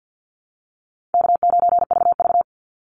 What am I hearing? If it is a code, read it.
Morse code, heard as a keyed tone, about 48 wpm, 700 Hz. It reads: K9VV
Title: Easy Call Signs - 50wpm